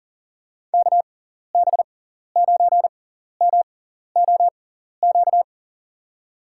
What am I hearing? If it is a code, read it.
Morse code, heard as a keyed tone, about 40 wpm, spaced as for 14 wpm, 700 Hz. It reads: KB9MOQ
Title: USA Call Signs - 6 Characters - 6x Character Spacing - 40wpm